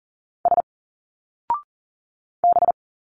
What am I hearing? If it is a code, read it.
Morse code, heard as a keyed tone, about 40 wpm, 700 Hz. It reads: S B
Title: Single Letters 40wpm